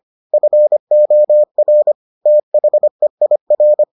Morse code, heard as a keyed tone, about 25 wpm, 600 Hz. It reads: FOR THEIR